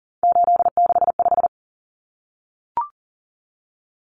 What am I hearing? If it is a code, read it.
Morse code, heard as a keyed tone, about 40 wpm, 700 Hz. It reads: 865
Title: Common Cut Numbers - 3-Digit 40wpm